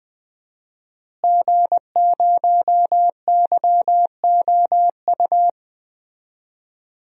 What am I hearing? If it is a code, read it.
Morse code, heard as a keyed tone, about 20 wpm, 700 Hz. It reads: G0YOU
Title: DXCC-100 Call Signs - 5 Characters - Repeated 2x - 20wpm